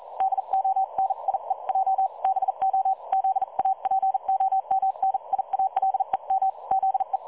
Morse code, heard as a keyed tone, about 40 wpm, 770 Hz. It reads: DONIE J BO ZAPOMNIALEM ZE
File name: dlazaawansowanych / nagraniazpasma / websdr recording 3565.0kHz